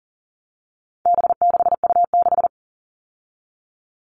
Morse code, heard as a keyed tone, about 40 wpm, 700 Hz. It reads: B6U6
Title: All Cut Numbers - 4-Digit 40wpm